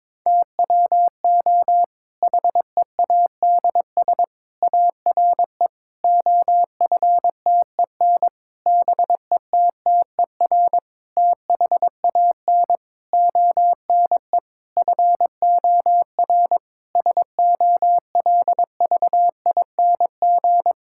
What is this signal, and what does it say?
Morse code, heard as a keyed tone, about 22 wpm, 700 Hz. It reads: TWO HEADS ARE OFTEN BETTER THAN ONE FOR SOLVING
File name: Difficult Sentences with Hints - 22wpm